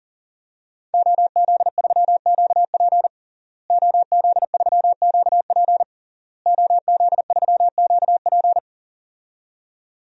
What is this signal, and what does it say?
Morse code, heard as a keyed tone, about 40 wpm, 700 Hz. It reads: OZ3QP OZ3QP OZ3QP
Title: DXCC-100 Call Signs - 5 Characters - Repeated 3x - 40wpm